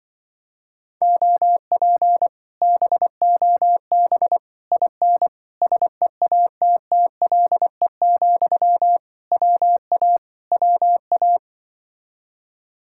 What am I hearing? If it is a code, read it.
Morse code, heard as a keyed tone, about 24 wpm, 700 Hz. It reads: OP BOB IN SEATTLE, WA WA
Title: Speed-Racing Common QSO Phrases - 20wpm